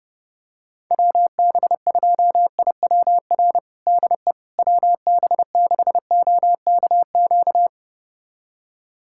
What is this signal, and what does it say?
Morse code, heard as a keyed tone, about 30 wpm, 700 Hz. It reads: WB2IWR DE WB6OKQ
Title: USA Callsign DE Callsign 30wpm